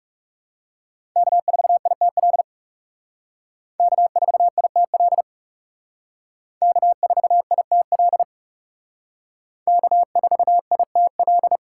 Morse code, heard as a keyed tone, about 36 wpm, 700 Hz. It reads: K4ITL K4ITL K4ITL K4ITL
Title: Speed Racing - USA 1x3 Call Signs 30wpm